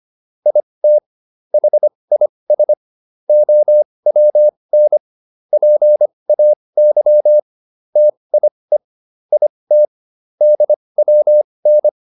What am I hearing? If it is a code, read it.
Morse code, heard as a keyed tone, about 25 wpm, spaced as for 20 wpm, 600 Hz. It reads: IT HIS OWN PAY TIE IT DWN